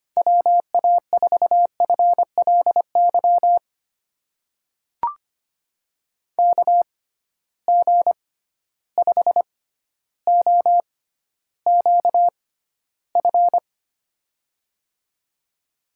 Morse code, heard as a keyed tone, about 25 wpm, spaced as for 13 wpm, 700 Hz. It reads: WA4FLY KG5OQF
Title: USA Call Signs - 6 Characters - 6x Character Spacing - 25wpm